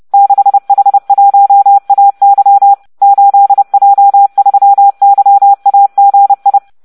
Morse code, heard as a keyed tone, about 30 wpm, 800 Hz. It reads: 6H1AY 8J3YAGI